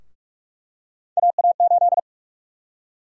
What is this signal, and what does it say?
Morse code, heard as a keyed tone, about 46 wpm, 700 Hz. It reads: AA8
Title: USA Call Sign Prefixes - 45wpm